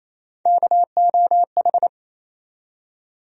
Morse code, heard as a keyed tone, about 28 wpm, 700 Hz. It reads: KOH
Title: USA Call Sign Postfixes - 3 Letters 28wpm